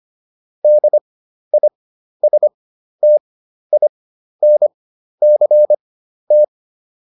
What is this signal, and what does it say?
Morse code, heard as a keyed tone, about 25 wpm, spaced as for 12 wpm, 600 Hz. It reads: DISTINCT